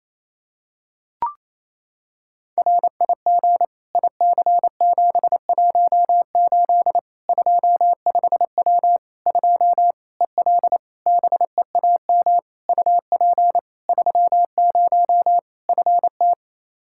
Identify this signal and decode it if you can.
Morse code, heard as a keyed tone, about 28 wpm, 700 Hz. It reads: RIG IC718 25W 2 EL BEAM UP 30 FT
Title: Practice QSO - Exchange 2 - Info - 28wpm